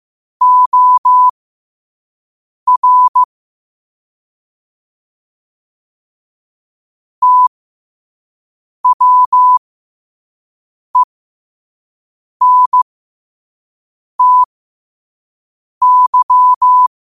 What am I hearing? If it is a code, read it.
Morse code, heard as a keyed tone, about 15 wpm, spaced as for 5 wpm, 1000 Hz. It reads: OR TWENTY
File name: qotdfiles / 5wpm